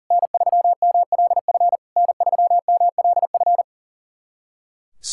Morse code, heard as a keyed tone, about 40 wpm, 700 Hz. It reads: N3MLF N3MLF
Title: SOTA QSO Practice 40wpm